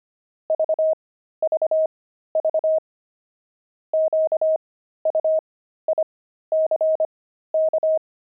Morse code, heard as a keyed tone, about 25 wpm, spaced as for 13 wpm, 630 Hz. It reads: VVV QUICK
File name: pangram-fs-13-cs-25-zebras